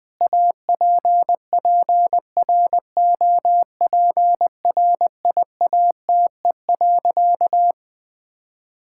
Morse code, heard as a keyed tone, about 20 wpm, 700 Hz. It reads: APPROPRIATE.
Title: Sentences from Top 1000 Words - Review Part 5 - 20wpm